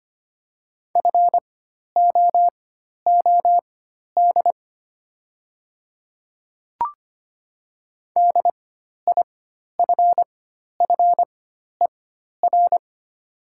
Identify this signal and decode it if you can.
Morse code, heard as a keyed tone, about 25 wpm, spaced as for 12 wpm, 700 Hz. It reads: FOOD DIFFER
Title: 4x Character Spacing - Top 500 Words - 25wpm